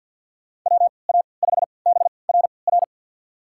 Morse code, heard as a keyed tone, about 49 wpm, spaced as for 28 wpm, 700 Hz. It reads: WA5BRR
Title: USA Call Signs - 6 Characters - 3x Character Spacing - 50wpm